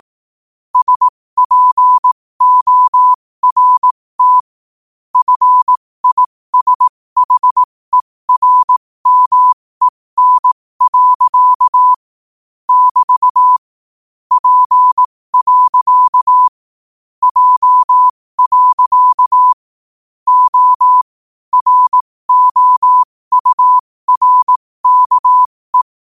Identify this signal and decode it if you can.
Morse code, heard as a keyed tone, about 18 wpm, spaced as for 15 wpm, 1000 Hz. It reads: SPORT FISHERMEN. <BT> P. J. O ROURKE
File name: qotdfiles / 15wpm